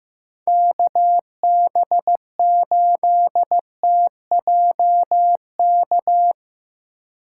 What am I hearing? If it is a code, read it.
Morse code, heard as a keyed tone, about 15 wpm, 700 Hz. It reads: KB8TJK
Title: USA Call Signs - 2x Character Spacing - 15wpm